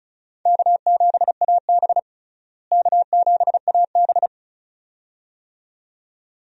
Morse code, heard as a keyed tone, about 36 wpm, 700 Hz. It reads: K7AB K7AB
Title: USA Call Signs - 4 Characters - Repeated 2x - 35wpm